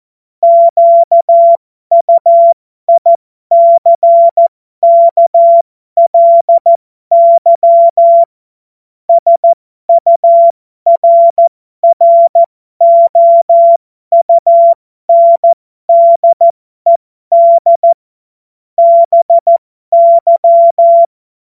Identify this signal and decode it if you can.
Morse code, heard as a keyed tone, about 14 wpm, spaced as for 12 wpm, 680 Hz. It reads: QUICKLY SURROUNDED BY
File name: Morsekurs / Witze